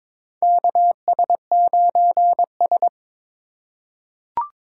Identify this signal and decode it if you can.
Morse code, heard as a keyed tone, about 22 wpm, 700 Hz. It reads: KS9S
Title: USA Call Signs - 4 Characters - 2x Character Spacing - 22wpm